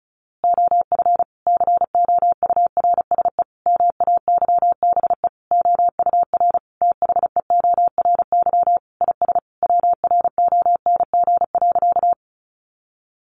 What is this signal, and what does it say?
Morse code, heard as a keyed tone, about 34 wpm, 700 Hz. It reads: OF COURSE MAYBE OUR THEORY IS WRONG.
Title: A Sound Of Thunder 35wpm